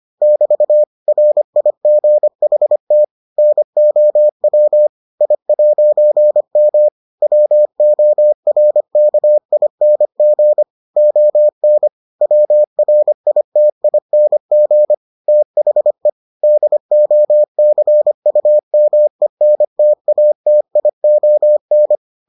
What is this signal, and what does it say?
Morse code, heard as a keyed tone, about 25 wpm, 600 Hz. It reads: <BT> RIGHT NOW I'M WORKING ON WRITING THE DOCUMENTATION